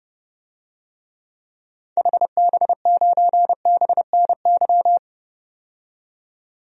Morse code, heard as a keyed tone, about 30 wpm, 700 Hz. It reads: HB9BNY
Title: DXCC-100 Call Signs - 6 Characters - 30wpm